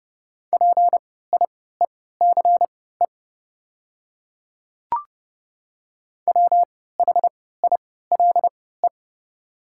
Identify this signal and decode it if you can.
Morse code, heard as a keyed tone, about 30 wpm, spaced as for 17 wpm, 700 Hz. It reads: PIECE WHILE
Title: 3x Character Spacing - Top 500 Words - 30wpm